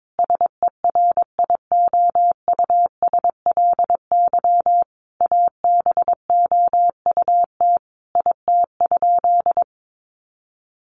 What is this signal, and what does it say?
Morse code, heard as a keyed tone, about 22 wpm, 700 Hz. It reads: SERIOUSLY ABOUT IT?
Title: Sentences from Top 1200 Words 22wpm